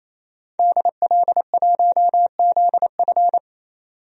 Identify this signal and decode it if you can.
Morse code, heard as a keyed tone, about 28 wpm, 700 Hz. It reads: DL1ZF
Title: DXCC-100 Call Signs - 5 Characters - 28wpm